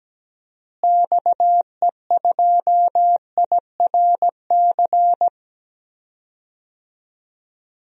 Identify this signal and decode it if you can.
Morse code, heard as a keyed tone, about 17 wpm, 700 Hz. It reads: XE2IRC
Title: DXCC-100 Call Signs - 6 Characters - Repeated 3x - 17wpm